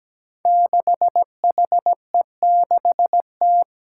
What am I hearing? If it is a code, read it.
Morse code, heard as a keyed tone, about 17 wpm, 700 Hz. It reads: 6HE6T
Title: Five Challenging Letter-Number Combinations 17wpm